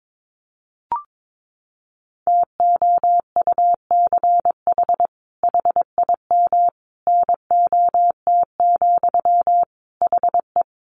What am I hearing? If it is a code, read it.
Morse code, heard as a keyed tone, about 22 wpm, 700 Hz. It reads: TOUCH HIM NOT, HE
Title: Sentences from Top 1200 Words 22wpm